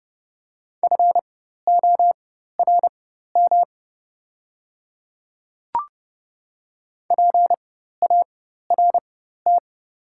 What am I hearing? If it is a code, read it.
Morse code, heard as a keyed tone, about 30 wpm, spaced as for 14 wpm, 700 Hz. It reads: FORM PART
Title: 4x Character Spacing - Top 300 Words - 30wpm